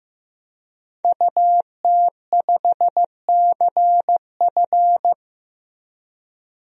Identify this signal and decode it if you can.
Morse code, heard as a keyed tone, about 15 wpm, 700 Hz. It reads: UT5CF
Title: DXCC-100 Call Signs - 5 Characters - 15wpm